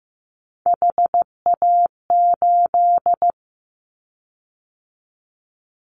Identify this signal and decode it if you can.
Morse code, heard as a keyed tone, about 15 wpm, 700 Hz. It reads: HA8
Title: Callsign Prefixes for Top 50 DXCC - 15wpm